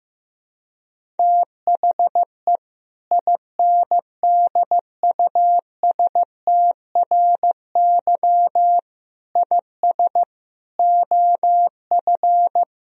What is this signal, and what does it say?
Morse code, heard as a keyed tone, about 15 wpm, 700 Hz. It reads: THE INDUSTRY IS OF
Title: Sentences from Top 1000 Words - Review Part 4 - 15wpm